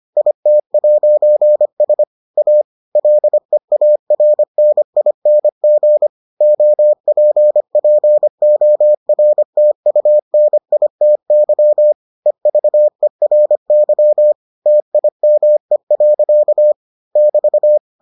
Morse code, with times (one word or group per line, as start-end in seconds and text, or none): 0.2-2.0 IT'S
2.4-2.6 A
3.0-6.1 LEARNING
6.4-11.9 OPPORTUNITY
12.3-14.3 EVERY
14.7-16.7 TIME.
17.2-17.8 <BT>